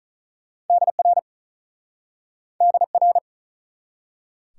Morse code, heard as a keyed tone, about 45 wpm, 700 Hz. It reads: DR DR